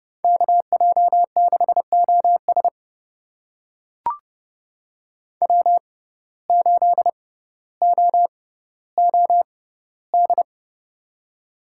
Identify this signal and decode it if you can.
Morse code, heard as a keyed tone, about 32 wpm, spaced as for 16 wpm, 700 Hz. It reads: KJ6OS W8OOD
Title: USA Call Signs - 5 Characters - 6x Character Spacing - 30wpm